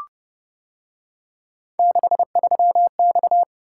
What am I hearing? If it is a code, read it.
Morse code, heard as a keyed tone, about 29 wpm, 700 Hz. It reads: 63X